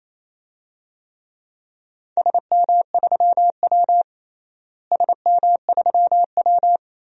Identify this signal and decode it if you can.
Morse code, heard as a keyed tone, about 28 wpm, 700 Hz. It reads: SM3W SM3W
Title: DXCC-100 Call Signs - 4 Characters - Repeated 2x - 28wpm